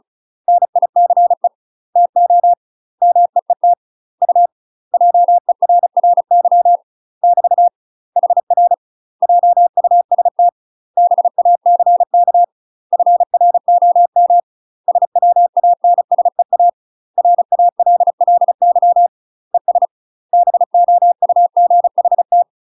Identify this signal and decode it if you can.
Morse code, heard as a keyed tone, about 34 wpm, 700 Hz. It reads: NICE TO MEET U JERRY <BT> HR JUST BACK FROM SWANSEA RALLY ES BOUGHT
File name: UK MASTER QSOs TRAINING / 35WPM / UK MASTER QSOs 0022